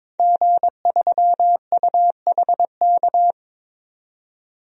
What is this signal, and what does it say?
Morse code, heard as a keyed tone, about 22 wpm, 700 Hz. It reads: G3UHK
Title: DXCC-100 Call Signs - 5 Characters - 22wpm